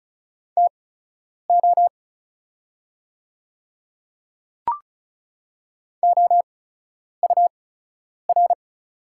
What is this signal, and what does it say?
Morse code, heard as a keyed tone, about 36 wpm, spaced as for 10 wpm, 690 Hz. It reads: TO OUR
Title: 8x Character Spacing - Top 200 Words - 35wpm